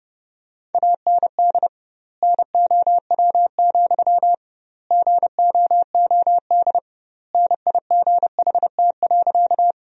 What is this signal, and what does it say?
Morse code, heard as a keyed tone, about 30 wpm, 700 Hz. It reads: AND NOW, GOOD NIGHT.
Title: 2x Word-Spacing - Sentences from Top 300 Words - 30wpm